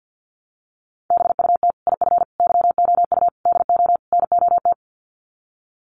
Morse code, heard as a keyed tone, about 50 wpm, 700 Hz. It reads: BUT IF YOU DO NOT
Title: Sets of 5 Words Encore 50wpm